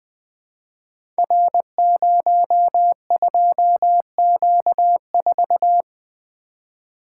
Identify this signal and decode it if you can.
Morse code, heard as a keyed tone, about 20 wpm, 700 Hz. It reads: R02Q4
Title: Code Groups - 5 Letter:Numbers - 20wpm